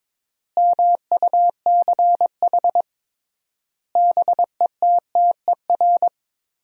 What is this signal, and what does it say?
Morse code, heard as a keyed tone, about 21 wpm, 700 Hz. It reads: MUCH BETTER
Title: 3x Word-Spacing - Sentences from Top 400 Words - 22wpm